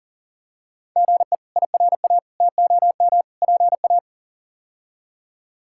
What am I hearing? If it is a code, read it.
Morse code, heard as a keyed tone, about 39 wpm, 700 Hz. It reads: GE IRA TOM PA
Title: K1USN SST QSO Practice - 40wpm